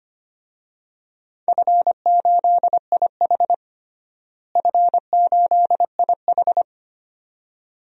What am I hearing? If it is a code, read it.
Morse code, heard as a keyed tone, about 25 wpm, 700 Hz. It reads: F8IH F8IH
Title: DXCC-100 Call Signs - 4 Characters - Repeated 2x - 25wpm